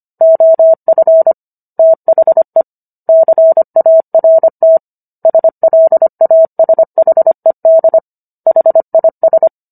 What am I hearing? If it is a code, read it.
Morse code, heard as a keyed tone, about 25 wpm, 650 Hz. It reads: OF THE CART SLASHED HIS